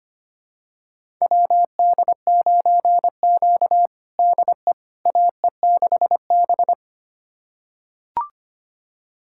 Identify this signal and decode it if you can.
Morse code, heard as a keyed tone, about 25 wpm, 700 Hz. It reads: WD9Q DE AE6B
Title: USA Callsign DE Callsign - 4 Characters - 25wpm